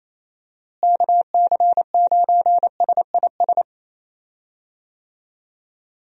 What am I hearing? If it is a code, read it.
Morse code, heard as a keyed tone, about 28 wpm, 700 Hz. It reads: KC9SIS
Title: USA Call Signs - 6 Characters - Repeated 2x - 28wpm